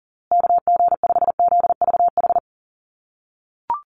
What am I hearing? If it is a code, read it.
Morse code, heard as a keyed tone, about 40 wpm, 700 Hz. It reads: KG5ZVH
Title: USA Call Signs Repeated 2x 40wpm